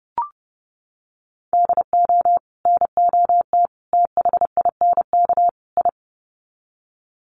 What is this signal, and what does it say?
Morse code, heard as a keyed tone, about 30 wpm, 700 Hz. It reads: DO NOT THINK I
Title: Sets of 4 Words 30wpm